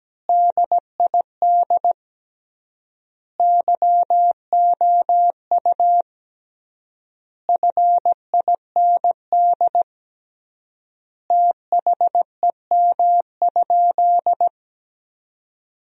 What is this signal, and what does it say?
Morse code, heard as a keyed tone, about 17 wpm, 700 Hz. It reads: DID YOU FIND THEM?
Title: 3x Word-Spacing - Sentences from Top 200 Words - 17wpm